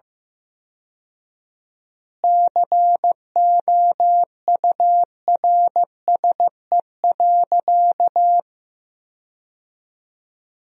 Morse code, heard as a keyed tone, about 15 wpm, 700 Hz. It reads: COURSE.
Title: 4x Word-Spacing - Sentences from Top 500 Words - 15wpm